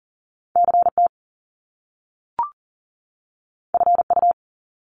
Morse code, heard as a keyed tone, about 40 wpm, 700 Hz. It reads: CT FU
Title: Top Two Letter Combinations in Words 40wpm